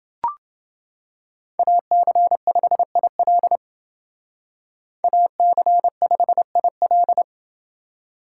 Morse code, heard as a keyed tone, about 29 wpm, 700 Hz. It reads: AC5IL AC5IL